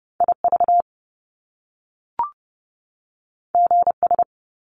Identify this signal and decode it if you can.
Morse code, heard as a keyed tone, about 30 wpm, 700 Hz. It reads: IV GS
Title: Top Two Letter Combinations in Words 30wpm